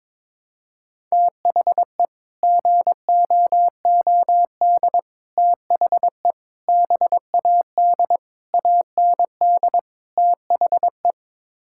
Morse code, heard as a keyed tone, about 22 wpm, 700 Hz. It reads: THE GOOD THE BAD AND THE
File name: practice / 22wpm / Sets of 6 Words - 22wpm